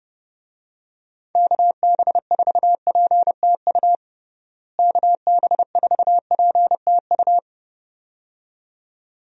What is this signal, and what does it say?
Morse code, heard as a keyed tone, about 30 wpm, 700 Hz. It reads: KB4PTU KB4PTU
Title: USA Call Signs - 6 Characters - Repeated 2x - 30wpm